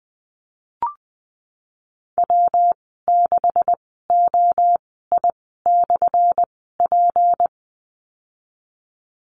Morse code, heard as a keyed tone, about 20 wpm, spaced as for 15 wpm, 700 Hz. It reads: W6OI/P
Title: 179 Lesson 29g - Slash - Call Signs 20wpm